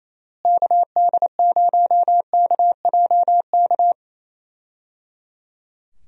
Morse code, heard as a keyed tone, about 28 wpm, 700 Hz. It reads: KD0KJK
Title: USA Call Signs - 6 Characters - Repeated 3x - 28wpm